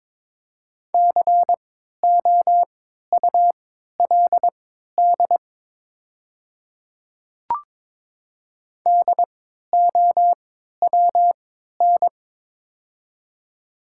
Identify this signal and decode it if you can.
Morse code, heard as a keyed tone, about 22 wpm, spaced as for 13 wpm, 700 Hz. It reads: COULD DOWN
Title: 3x Character Spacing - Top 100 Words - 22wpm